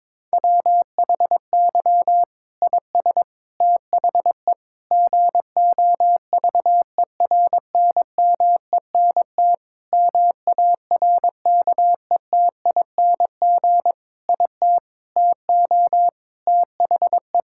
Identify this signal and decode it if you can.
Morse code, heard as a keyed tone, about 21 wpm, 700 Hz. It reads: WHY IS THE GOVERNMENT MARKETING IT TO THE
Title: Sentences from Top 1000 Words - Review Part 5 - 22wpm